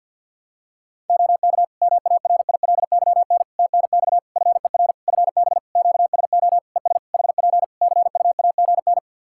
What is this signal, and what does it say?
Morse code, heard as a keyed tone, about 48 wpm, 700 Hz. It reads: OK MARILYN TNX FER FB QSO ES HP CUAGN
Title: Practice QSO - Exchange 4 - Ending - 50wpm